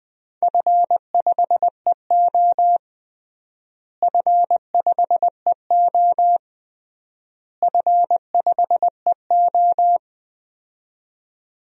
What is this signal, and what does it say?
Morse code, heard as a keyed tone, about 20 wpm, 700 Hz. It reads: F5EO F5EO F5EO
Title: DXCC-100 Call Signs - 4 Characters - Repeated 3x - 20wpm